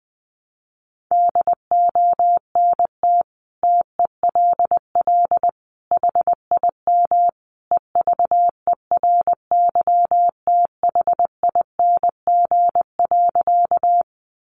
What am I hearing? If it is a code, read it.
Morse code, heard as a keyed tone, about 20 wpm, 700 Hz. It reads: DONT TELL HIM EVERYTHING.
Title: Sentences from Top 700 Words 20wpm